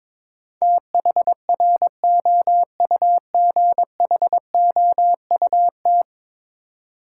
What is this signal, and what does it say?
Morse code, heard as a keyed tone, about 22 wpm, 700 Hz. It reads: THROUGHOUT
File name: practice / 22wpm / 10 Letter Words - First 3 Letters Given - 22wpm